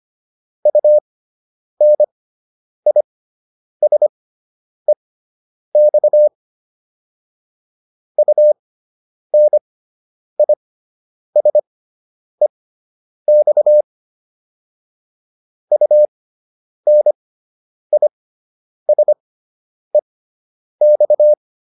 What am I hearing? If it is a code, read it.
Morse code, heard as a keyed tone, about 25 wpm, spaced as for 9 wpm, 600 Hz. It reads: UNISEX UNISEX UNISEX